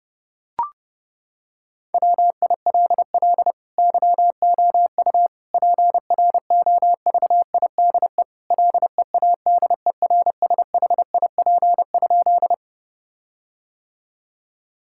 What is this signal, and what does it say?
Morse code, heard as a keyed tone, about 30 wpm, 700 Hz. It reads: WILL YOU PROVIDE LEADERSHIP?
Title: Sentences from Top 1600 Words 30wpm